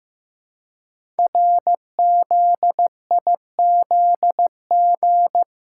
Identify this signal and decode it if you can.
Morse code, heard as a keyed tone, about 15 wpm, 700 Hz. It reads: RZIZG
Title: Code Groups - 5 Letters - 15wpm